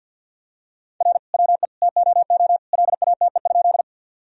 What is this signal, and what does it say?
Morse code, heard as a keyed tone, about 50 wpm, 700 Hz. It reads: R WE TOO LATE?
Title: Sentences from Top 300 Words and Abbreviations 50wpm